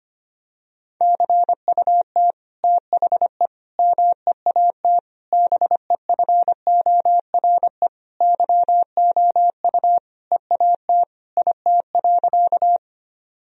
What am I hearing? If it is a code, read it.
Morse code, heard as a keyed tone, about 25 wpm, 700 Hz. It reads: CUT THE MEAT BEFORE YOU EAT IT.
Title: Sentences from Top 1000 Words - Review Part 3 - 25wpm